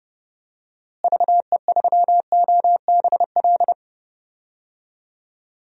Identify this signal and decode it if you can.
Morse code, heard as a keyed tone, about 30 wpm, 700 Hz. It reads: VE3OBL
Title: DXCC-100 Call Signs - 6 Characters - Repeated 2x - 30wpm